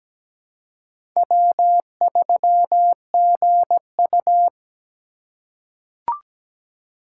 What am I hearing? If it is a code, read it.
Morse code, heard as a keyed tone, about 17 wpm, 700 Hz. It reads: W3GU
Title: USA Call Signs - 4 Characters - 2x Character Spacing - 17wpm